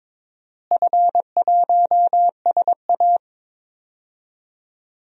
Morse code, heard as a keyed tone, about 22 wpm, 700 Hz. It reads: F1SA
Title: DXCC-100 Call Signs - 4 Characters - Repeated 3x - 22wpm